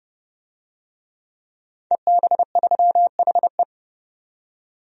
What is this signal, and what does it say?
Morse code, heard as a keyed tone, about 30 wpm, 700 Hz. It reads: EB3HE
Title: DXCC-100 Call Signs - 5 Characters - 30wpm